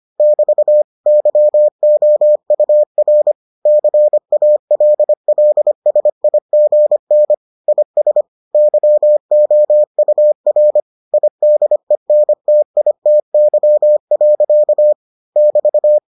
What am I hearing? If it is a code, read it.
Morse code, heard as a keyed tone, about 25 wpm, 600 Hz. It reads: <BT> YOUR CALLSIGN IS YOUR IDENTITY. <BT>